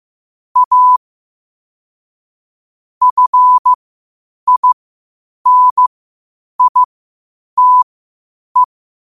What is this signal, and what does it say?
Morse code, heard as a keyed tone, about 15 wpm, spaced as for 8 wpm, 1000 Hz. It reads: A FINITE
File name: 8wpm